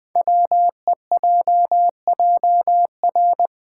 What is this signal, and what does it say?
Morse code, heard as a keyed tone, about 20 wpm, 700 Hz. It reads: WEJJR
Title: Code Groups - 5 Letters - 20wpm